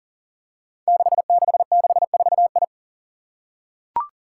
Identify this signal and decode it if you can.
Morse code, heard as a keyed tone, about 41 wpm, 700 Hz. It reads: 6664I
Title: Five Challenging Letter-Number Combinations 40wpm